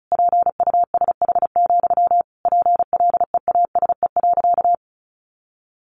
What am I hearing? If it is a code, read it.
Morse code, heard as a keyed tone, about 34 wpm, 700 Hz. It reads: PUSH, PLEASE.
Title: Sentences from Top 1500 Words 35wpm